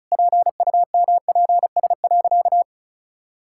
Morse code, heard as a keyed tone, about 36 wpm, 700 Hz. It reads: PUMPS.
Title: Pangram Challenge v2 - 35wpm